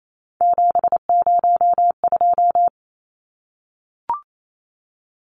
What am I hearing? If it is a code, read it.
Morse code, heard as a keyed tone, about 28 wpm, 700 Hz. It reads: 702
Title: Numbers - 3-Digit 28wpm